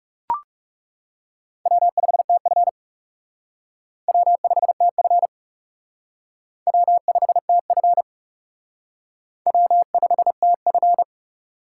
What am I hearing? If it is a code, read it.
Morse code, heard as a keyed tone, about 36 wpm, 700 Hz. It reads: W5TF W5TF W5TF W5TF
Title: Speed Racing - USA Call Signs 30wpm